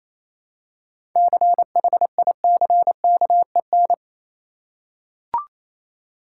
Morse code, heard as a keyed tone, about 28 wpm, 700 Hz. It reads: CHICKEN
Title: Seven Letter Words - 28wpm